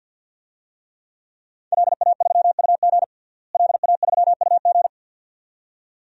Morse code, heard as a keyed tone, about 52 wpm, 700 Hz. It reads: LA3UG LA3UG
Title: DXCC-100 Call Signs - 5 Characters - Repeated 2x - 50wpm